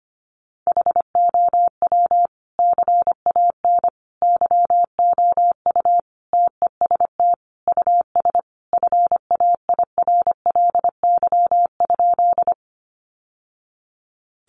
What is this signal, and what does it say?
Morse code, heard as a keyed tone, about 25 wpm, 700 Hz. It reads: HOW CAN YOU TEST US FAIRLY?
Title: Sentences from Top 1100 Words 25wpm